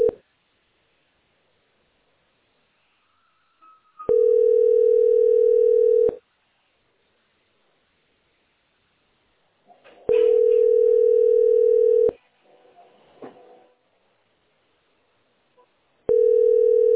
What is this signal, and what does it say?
Ringback tone (North America), 440 + 480 Hz.